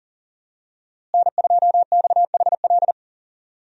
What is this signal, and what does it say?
Morse code, heard as a keyed tone, about 40 wpm, 700 Hz. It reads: N2XHL
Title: USA Call Signs - 5 Characters - Repeated 2x - 40wpm